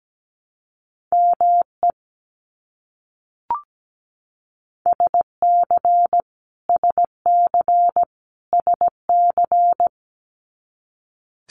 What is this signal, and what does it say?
Morse code, heard as a keyed tone, about 17 wpm, 700 Hz. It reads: ME SC SC SC
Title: US States Repeated 3x 17wpm